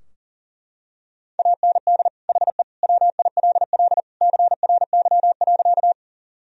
Morse code, heard as a keyed tone, about 40 wpm, 700 Hz. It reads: AND HE WILL CRY.
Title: Sentences from Top 1000 Words - Review Part 2 - 40wpm